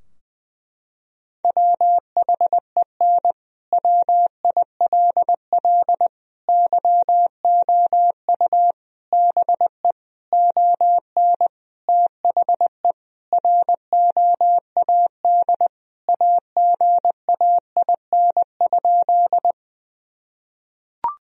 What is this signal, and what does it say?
Morse code, heard as a keyed tone, about 20 wpm, 700 Hz. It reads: WHEN WILL YOU BE ON THE ROAD AGAIN?
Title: Sentences from Top 1000 Words - Review Part 2 - 20wpm